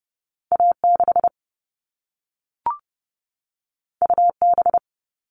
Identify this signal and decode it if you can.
Morse code, heard as a keyed tone, about 30 wpm, 700 Hz. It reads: A 6 U B
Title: All Cut Numbers - 2-Digit 30wpm